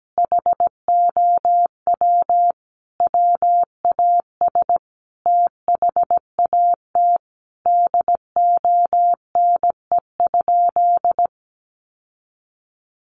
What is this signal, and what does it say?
Morse code, heard as a keyed tone, about 17 wpm, 700 Hz. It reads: HOW WAS THAT DONE?
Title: Sentences from Top 1100 Words 17wpm